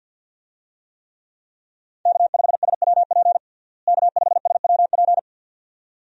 Morse code, heard as a keyed tone, about 47 wpm, 700 Hz. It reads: K5SWP K5SWP
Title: USA Call Signs - 5 Characters - Repeated 2x - 50wpm